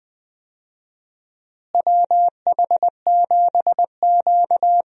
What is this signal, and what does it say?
Morse code, heard as a keyed tone, about 21 wpm, 700 Hz. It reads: WH7Q
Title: USA Call Signs - 4 Characters - Repeated 3x - 20wpm